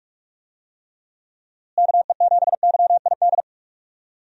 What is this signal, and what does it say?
Morse code, heard as a keyed tone, about 46 wpm, 700 Hz. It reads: KE7YID
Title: USA Call Signs - 6 Characters - Repeated 3x - 45wpm